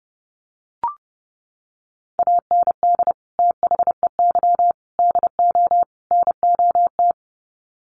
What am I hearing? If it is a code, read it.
Morse code, heard as a keyed tone, about 29 wpm, 700 Hz. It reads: AND THEY DO NOT
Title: Sets of 4 Words 30wpm